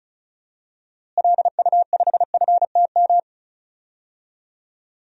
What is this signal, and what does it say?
Morse code, heard as a keyed tone, about 36 wpm, 700 Hz. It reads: LU5FTM
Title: DXCC-100 Call Signs - 6 Characters - Repeated 2x - 35wpm